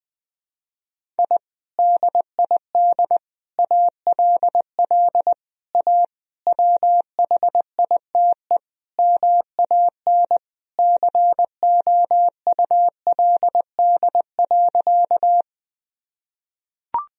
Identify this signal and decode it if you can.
Morse code, heard as a keyed tone, about 20 wpm, 700 Hz. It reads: I DID ALL A WHITE MAN COULD.
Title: Sentences from Top 1000 Words - Review Part 1 - 20wpm